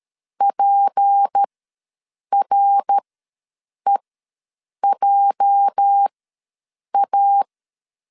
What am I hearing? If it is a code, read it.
Morse code, heard as a keyed tone, about 13 wpm, spaced as for 7 wpm, 800 Hz. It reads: PREJA
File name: LEZIONI / 8 YJ45 / 8 YJ4 ws 12wpm 3249